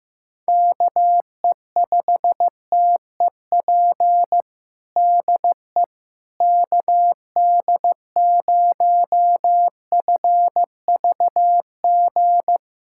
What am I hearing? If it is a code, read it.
Morse code, heard as a keyed tone, about 15 wpm, 700 Hz. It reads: KE5TEP DE KD0FVG
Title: USA Callsign DE Callsign - 6 Characters - 15wpm